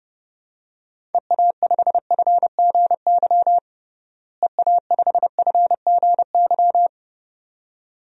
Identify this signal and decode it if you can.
Morse code, heard as a keyed tone, about 30 wpm, 700 Hz. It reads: EA5FGY EA5FGY
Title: DXCC-100 Call Signs - 6 Characters - Repeated 2x - 30wpm